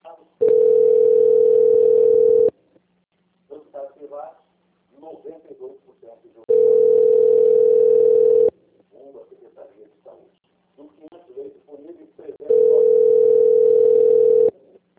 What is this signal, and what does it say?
Ringback tone (North America), 440 + 480 Hz, with speech.
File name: out- - - - - .28462